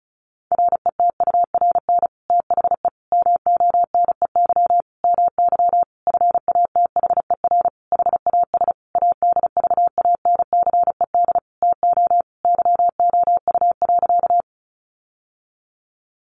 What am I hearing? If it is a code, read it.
Morse code, heard as a keyed tone, about 34 wpm, 700 Hz. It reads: RETURN THE MONEY MY FATHER HAS ADVANCED TO YOU.
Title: Sentences from Top 1200 Words 35wpm